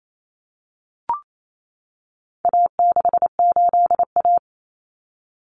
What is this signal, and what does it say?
Morse code, heard as a keyed tone, about 28 wpm, 700 Hz. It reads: A68A
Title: Common Cut Numbers - 4-Digit 28wpm